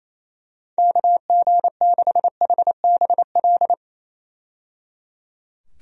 Morse code, heard as a keyed tone, about 28 wpm, 700 Hz. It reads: KG6HBL
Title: USA Call Signs - 6 Characters - 28wpm